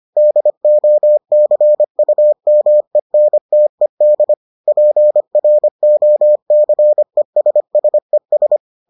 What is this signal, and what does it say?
Morse code, heard as a keyed tone, about 25 wpm, 600 Hz. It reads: DOCUMENTED PROCESSES